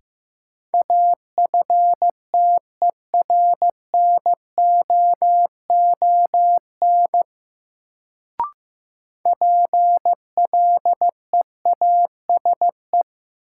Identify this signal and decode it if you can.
Morse code, heard as a keyed tone, about 15 wpm, 700 Hz. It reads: AFTERNOON PLEASE
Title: Sentences from Top 1500 Words - New Word Drill - 15wpm